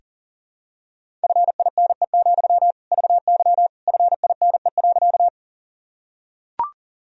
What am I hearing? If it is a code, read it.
Morse code, heard as a keyed tone, about 40 wpm, 700 Hz. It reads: FINE, VY FINE.
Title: Sentences from Top 500 Words and Abbreviations 40wpm